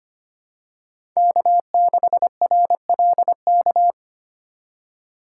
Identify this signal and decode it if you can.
Morse code, heard as a keyed tone, about 25 wpm, 700 Hz. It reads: K6RLK
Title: USA Call Signs - 5 Characters - 2x Character Spacing - 25wpm